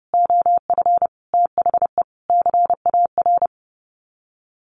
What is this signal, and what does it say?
Morse code, heard as a keyed tone, about 30 wpm, 700 Hz. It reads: OF THE CAR
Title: Sets of 5 Words Encore 30wpm